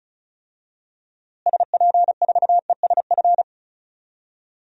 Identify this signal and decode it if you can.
Morse code, heard as a keyed tone, about 36 wpm, 700 Hz. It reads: SP4ESF
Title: DXCC-100 Call Signs - 6 Characters - Repeated 3x - 35wpm